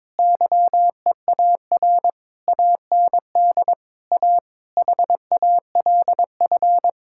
Morse code, heard as a keyed tone, about 22 wpm, 700 Hz. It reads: YEAR AND A HALF
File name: Sentences with 7-Word Phrases - 22wpm